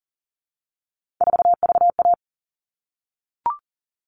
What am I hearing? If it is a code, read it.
Morse code, heard as a keyed tone, about 40 wpm, 700 Hz. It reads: 4VA
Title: All Cut Numbers - 3-Digit 40wpm